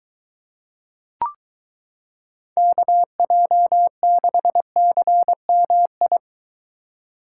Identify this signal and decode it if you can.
Morse code, heard as a keyed tone, about 23 wpm, 700 Hz. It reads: KJ6CMI